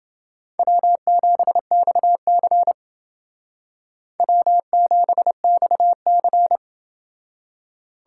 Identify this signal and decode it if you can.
Morse code, heard as a keyed tone, about 29 wpm, 700 Hz. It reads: W7XC W7XC